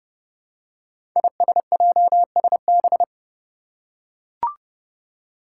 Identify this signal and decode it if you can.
Morse code, heard as a keyed tone, about 30 wpm, 700 Hz. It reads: ISJSB
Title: Five Challenging Letter-Number Combinations 30wpm